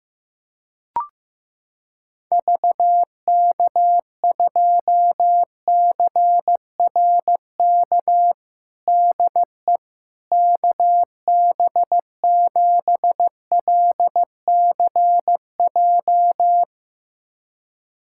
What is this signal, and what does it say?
Morse code, heard as a keyed tone, about 15 wpm, 700 Hz. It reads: VK2CRK DE KB7LCJ
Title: USA Callsign DE Callsign 15wpm